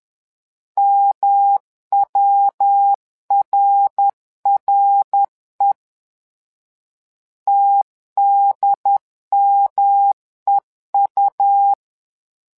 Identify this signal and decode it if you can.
Morse code, heard as a keyed tone, about 10 wpm, 800 Hz. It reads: MWRRE TDMEU
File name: LEZIONI / 3 MWRO / 3 MWRO 10wpm 2148